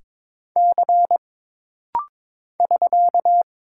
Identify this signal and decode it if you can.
Morse code, heard as a keyed tone, about 22 wpm, 700 Hz. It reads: C <SK>
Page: This rapid-fire timing is geared to push your ICR (Instant Character Recognition) skills.